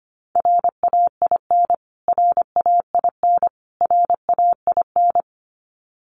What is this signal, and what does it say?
Morse code, heard as a keyed tone, about 25 wpm, 700 Hz. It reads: RAIN RAIN RAIN
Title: Top 300 Words Repeated 25wpm